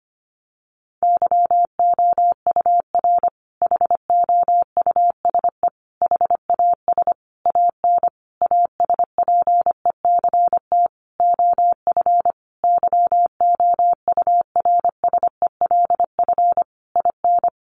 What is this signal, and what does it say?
Morse code, heard as a keyed tone, about 25 wpm, 700 Hz. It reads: YOUR HOUSE HAS AN ASPECT OF YOURSELF IN
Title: Sentences from Top 1300 Words 25wpm